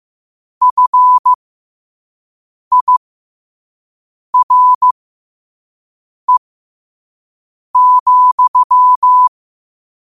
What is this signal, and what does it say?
Morse code, heard as a keyed tone, about 15 wpm, spaced as for 5 wpm, 1000 Hz. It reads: FIRE,
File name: qotdfiles / 5wpm